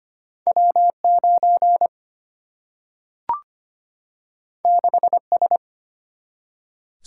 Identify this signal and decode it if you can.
Morse code, heard as a keyed tone, about 25 wpm, 700 Hz. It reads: W9 6S
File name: Two Letter-Number Combinations - 25wpm